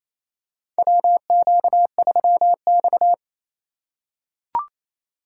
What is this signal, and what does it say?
Morse code, heard as a keyed tone, about 28 wpm, 700 Hz. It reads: WQ3X
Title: USA Call Signs - 4 Characters - 2x Character Spacing - 28wpm